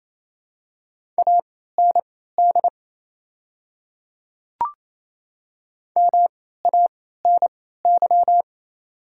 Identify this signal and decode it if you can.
Morse code, heard as a keyed tone, about 28 wpm, spaced as for 16 wpm, 690 Hz. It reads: AND MANY
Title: 3x Character Spacing - Top 100 Words - 28wpm